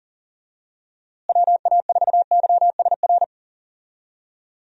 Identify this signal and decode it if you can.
Morse code, heard as a keyed tone, about 39 wpm, 700 Hz. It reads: WA4YSR
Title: USA Call Signs - 6 Characters - Repeated 3x - 40wpm